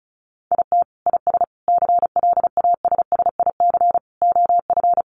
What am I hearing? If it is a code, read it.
Morse code, heard as a keyed tone, about 34 wpm, 700 Hz. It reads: IT IS CLASSIC OF
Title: Sentences from Top 1000 Words 35wpm